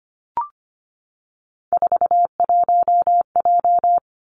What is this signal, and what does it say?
Morse code, heard as a keyed tone, about 25 wpm, 700 Hz. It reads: 41J
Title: Three Challenging Letter-Number Combinations 25wpm